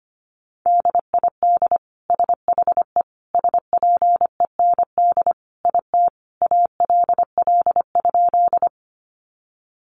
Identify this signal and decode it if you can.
Morse code, heard as a keyed tone, about 25 wpm, 700 Hz. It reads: DID SHE SPEND IT ALL?
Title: Sentences from Top 800 Words 25wpm